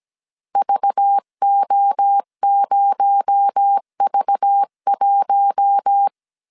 Morse code, heard as a keyed tone, about 17 wpm, 800 Hz. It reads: VO0V1
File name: CW / DET / 1 9 1500C S C 16wpm 2909